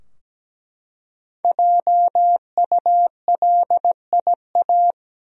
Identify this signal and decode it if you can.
Morse code, heard as a keyed tone, about 17 wpm, 700 Hz. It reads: JULIA
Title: USA Common Names - Max 5 Letters - 17wpm